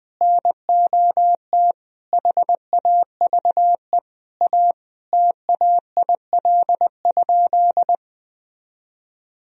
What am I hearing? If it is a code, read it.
Morse code, heard as a keyed tone, about 20 wpm, 700 Hz. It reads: NOT HAVE A TAIL?
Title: Sentences from Top 1000 Words - Review Part 3 - 20wpm